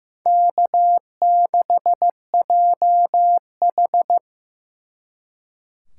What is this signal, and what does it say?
Morse code, heard as a keyed tone, about 18 wpm, 700 Hz. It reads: K6JH
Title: Speed Racing - USA 1x2 Call Signs 15wpm